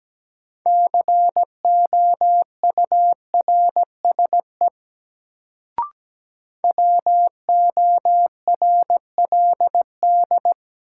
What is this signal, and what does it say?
Morse code, heard as a keyed tone, about 17 wpm, 700 Hz. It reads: COURSE WORLD
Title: Sets of 2 Words - New Word Drill - 17wpm